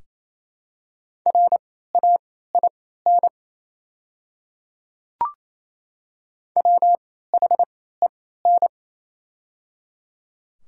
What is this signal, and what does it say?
Morse code, heard as a keyed tone, about 28 wpm, spaced as for 16 wpm, 700 Hz. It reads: RAIN WHEN
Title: 3x Character Spacing - Top 300 Words - 28wpm